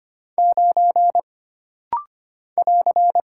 Morse code, heard as a keyed tone, about 25 wpm, 700 Hz. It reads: 9 <AR>
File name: Single Character (Letter:Number:Punctuation:Prosign) - Rapid-Fire - 25wpm